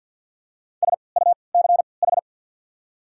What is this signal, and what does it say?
Morse code, heard as a keyed tone, about 54 wpm, spaced as for 29 wpm, 690 Hz. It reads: SUCH
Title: 3x Character Spacing - Top 200 Words - 50wpm